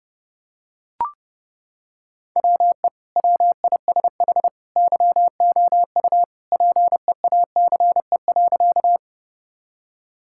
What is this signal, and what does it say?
Morse code, heard as a keyed tone, about 30 wpm, 700 Hz. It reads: WE WISH YOU PEACE.
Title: Sentences from Top 1600 Words 30wpm